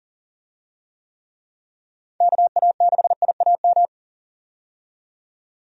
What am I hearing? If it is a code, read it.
Morse code, heard as a keyed tone, about 38 wpm, 700 Hz. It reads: KA6IAM
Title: USA Call Signs - 6 Characters - 40wpm